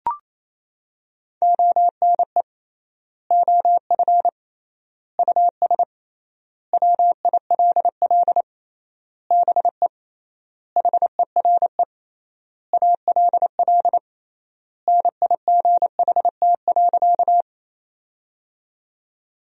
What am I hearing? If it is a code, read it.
Morse code, heard as a keyed tone, about 28 wpm, 700 Hz. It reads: ONE OF US WILL BE HERE ALL NIGHT.
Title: 3x Word-Spacing - Sentences from Top 300 Words - 28wpm